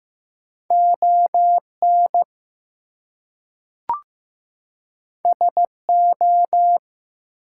Morse code, heard as a keyed tone, about 15 wpm, 700 Hz. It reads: ON SO
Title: Two Letter Words - 15wpm